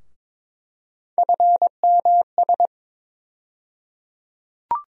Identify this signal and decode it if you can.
Morse code, heard as a keyed tone, about 22 wpm, 700 Hz. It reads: FMS
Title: USA Call Sign Postfixes - 3 Letters 22wpm